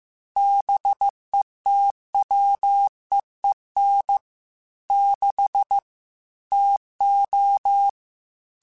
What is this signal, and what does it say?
Morse code, heard as a keyed tone, about 15 wpm, 800 Hz. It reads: BETWEEN 6 TO
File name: Weather Sets / Weather Sets-mp3s / Weather 4 24 @13